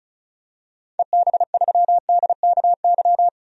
Morse code, heard as a keyed tone, about 36 wpm, 700 Hz. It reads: EB3DKY
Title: DXCC-100 Call Signs - 6 Characters - 35wpm